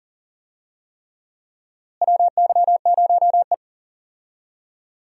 Morse code, heard as a keyed tone, about 40 wpm, 700 Hz. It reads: WY0E
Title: USA Call Signs - 4 Characters - Repeated 3x - 40wpm